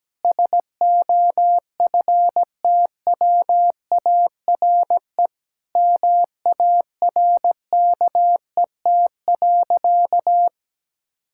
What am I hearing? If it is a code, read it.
Morse code, heard as a keyed tone, about 17 wpm, 700 Hz. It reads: SOFTWARE MARKET.
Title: Sentences from Top 1000 Words - Review Part 6 - 17wpm